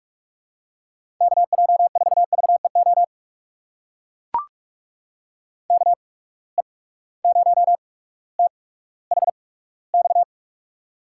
Morse code, heard as a keyed tone, about 45 wpm, spaced as for 19 wpm, 700 Hz. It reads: KJ4VEO KE0THX
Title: USA Call Signs - 6 Characters - 8x Character Spacing - 45wpm